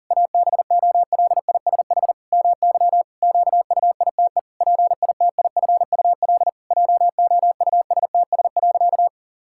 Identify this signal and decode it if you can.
Morse code, heard as a keyed tone, about 40 wpm, 700 Hz. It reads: ABOLISH MY QUITE PITIFUL JOUSTS.
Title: Pangram Challenge v2 - 40wpm